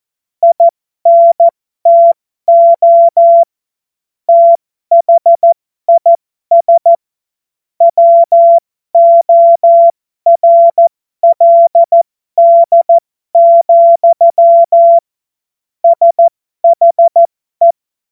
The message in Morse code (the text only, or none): INTO THIS WORLD, SHE